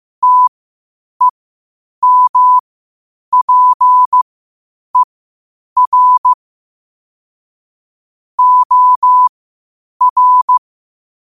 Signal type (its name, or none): Morse code